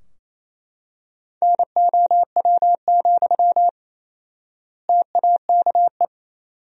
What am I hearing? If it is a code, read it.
Morse code, heard as a keyed tone, about 28 wpm, 700 Hz. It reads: NOW, TAKE
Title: 4x Word-Spacing - Sentences from Top 400 Words - 28wpm